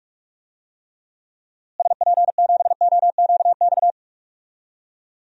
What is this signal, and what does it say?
Morse code, heard as a keyed tone, about 46 wpm, 700 Hz. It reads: SP7OQX
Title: DXCC-100 Call Signs - 6 Characters - Repeated 2x - 45wpm